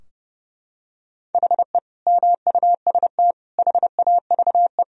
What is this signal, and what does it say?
Morse code, heard as a keyed tone, about 30 wpm, 700 Hz. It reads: HE MUST HAVE
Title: Sentences from Top 1000 Words - Review Part 4 - 30wpm